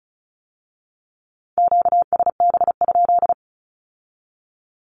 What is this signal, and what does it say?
Morse code, heard as a keyed tone, about 36 wpm, 700 Hz. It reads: QSB?
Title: Q-Codes with Spoken Meaning - 35wpm